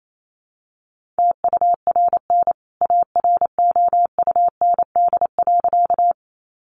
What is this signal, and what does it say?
Morse code, heard as a keyed tone, about 28 wpm, 700 Hz. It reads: TURN AROUND.
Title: A Sound Of Thunder 28wpm